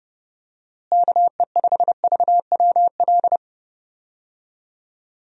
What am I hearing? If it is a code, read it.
Morse code, heard as a keyed tone, about 31 wpm, 700 Hz. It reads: KE5VWL
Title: USA Call Signs - 6 Characters - Repeated 2x - 30wpm